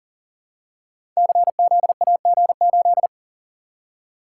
Morse code, heard as a keyed tone, about 39 wpm, 700 Hz. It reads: CZAG8
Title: Code Groups - 5 Letter:Numbers - 40wpm